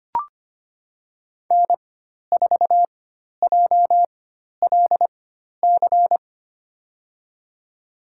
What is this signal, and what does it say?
Morse code, heard as a keyed tone, about 25 wpm, spaced as for 12 wpm, 700 Hz. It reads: N4JLC
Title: USA Call Signs - 5 Characters - 4x Character Spacing - 25wpm